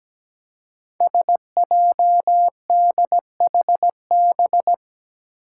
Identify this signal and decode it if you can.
Morse code, heard as a keyed tone, about 17 wpm, 700 Hz. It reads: SJDHB
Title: Five Challenging Letter-Number Combinations 17wpm